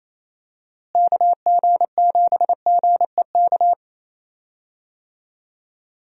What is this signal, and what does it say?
Morse code, heard as a keyed tone, about 28 wpm, 700 Hz. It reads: KG7GEK
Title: USA Call Signs - 6 Characters - Repeated 2x - 28wpm